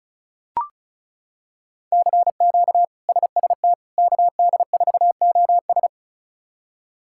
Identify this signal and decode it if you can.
Morse code, heard as a keyed tone, about 36 wpm, 700 Hz. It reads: CQ SST KD4OS
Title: K1USN SST QSO Practice - 35wpm